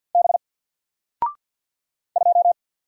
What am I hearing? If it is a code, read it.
Morse code, heard as a keyed tone, about 50 wpm, 700 Hz. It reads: B 2
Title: Single Character (Letter:Number:Punctuation:Prosign) 50wpm